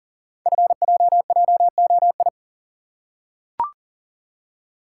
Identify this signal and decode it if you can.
Morse code, heard as a keyed tone, about 39 wpm, 700 Hz. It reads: FJJOI
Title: Code Groups - 5 Letters - 40wpm